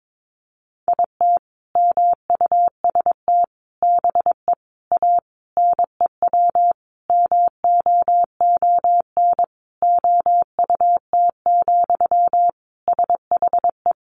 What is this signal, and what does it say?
Morse code, heard as a keyed tone, about 22 wpm, 700 Hz. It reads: IT MUST BE A NEW MOON OUT, SHE
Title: Sentences from Top 500 Words 22wpm